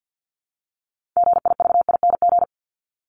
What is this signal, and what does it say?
Morse code, heard as a keyed tone, about 50 wpm, 700 Hz. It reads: GIVING
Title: ING Suffix - Top 100 Words with Base Words 50wpm